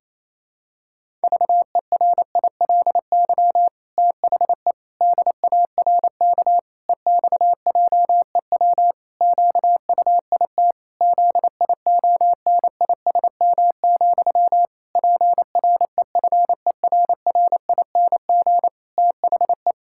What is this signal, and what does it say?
Morse code, heard as a keyed tone, about 28 wpm, 700 Hz. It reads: VERILY THE DARK EXJEW QUIT ZIONISM, PREFERRING THE
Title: Pangram Challenge v2 - 28wpm